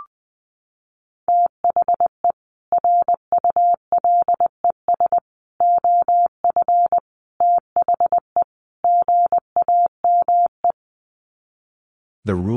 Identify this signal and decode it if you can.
Morse code, heard as a keyed tone, about 20 wpm, 700 Hz. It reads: THE RULES OF THE GAME
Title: Sets of 5 Words Encore2 20wpm